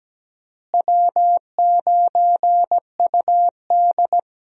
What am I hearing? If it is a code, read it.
Morse code, heard as a keyed tone, about 17 wpm, 700 Hz. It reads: W9UD
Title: USA Call Signs - 4 Characters - Repeated 3x - 17wpm